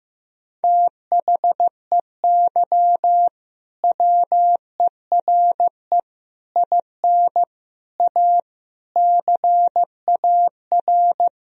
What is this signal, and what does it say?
Morse code, heard as a keyed tone, about 15 wpm, 700 Hz. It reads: THEY WERE IN A CAR
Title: Sentences from Top 1000 Words - Review Part 1 - 15wpm